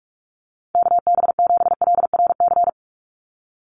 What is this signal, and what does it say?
Morse code, heard as a keyed tone, about 44 wpm, 700 Hz. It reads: KB7LRC
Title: USA Call Signs 45wpm